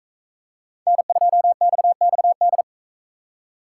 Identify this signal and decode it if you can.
Morse code, heard as a keyed tone, about 40 wpm, 700 Hz. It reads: N2XXD